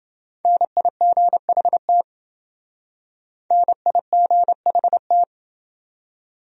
Morse code, heard as a keyed tone, about 28 wpm, 700 Hz. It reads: NIGHT NIGHT